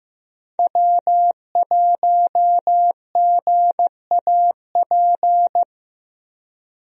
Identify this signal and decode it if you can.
Morse code, heard as a keyed tone, about 15 wpm, 700 Hz. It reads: W1GAP
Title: USA Call Signs - 5 Characters - 15wpm